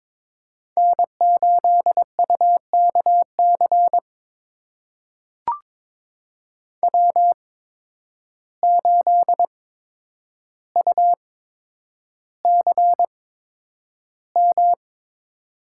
Morse code, heard as a keyed tone, about 22 wpm, spaced as for 9 wpm, 700 Hz. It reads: N8UKC W8UCM
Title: USA Call Signs - 5 Characters - 8x Character Spacing - 22wpm